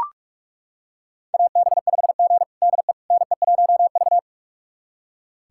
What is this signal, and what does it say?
Morse code, heard as a keyed tone, about 42 wpm, 700 Hz. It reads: AB5G DE NE1V
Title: USA Callsign DE Callsign - 4 Characters - 45wpm